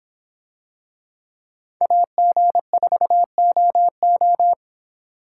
Morse code, heard as a keyed tone, about 26 wpm, 700 Hz. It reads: AG4OO